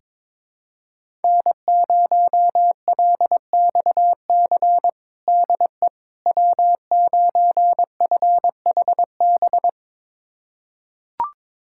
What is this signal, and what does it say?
Morse code, heard as a keyed tone, about 22 wpm, 700 Hz. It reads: N0LXC DE W9FHB
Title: USA Callsign DE Callsign - 5 Characters - 22wpm